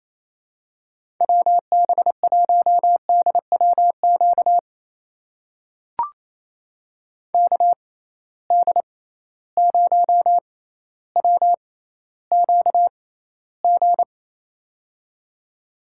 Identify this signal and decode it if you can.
Morse code, heard as a keyed tone, about 28 wpm, spaced as for 14 wpm, 700 Hz. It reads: WB1DWQ KD0WQG
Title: USA Call Signs - 6 Characters - 6x Character Spacing - 28wpm